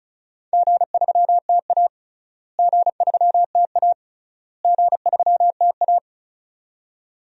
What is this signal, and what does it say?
Morse code, heard as a keyed tone, about 34 wpm, 700 Hz. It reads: G3TA G3TA G3TA
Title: DXCC-100 Call Signs - 4 Characters - Repeated 3x - 35wpm